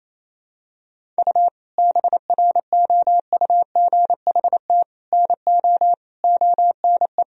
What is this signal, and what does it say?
Morse code, heard as a keyed tone, about 28 wpm, 700 Hz. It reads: U BROUGHT NO ONE
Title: Sentences from Top 500 Words and Abbreviations 28wpm